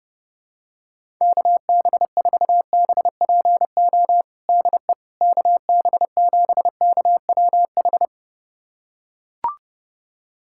Morse code, heard as a keyed tone, about 32 wpm, 700 Hz. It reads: KB4BPO DE KB7KWH
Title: USA Callsign DE Callsign - 6 Characters - 30wpm